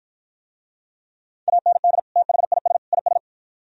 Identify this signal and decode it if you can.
Morse code, heard as a keyed tone, about 52 wpm, 700 Hz. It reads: AND THIS IS